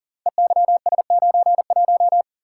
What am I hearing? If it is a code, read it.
Morse code, heard as a keyed tone, about 39 wpm, 700 Hz. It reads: EYS91
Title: Code Groups - 5 Letter:Numbers - 40wpm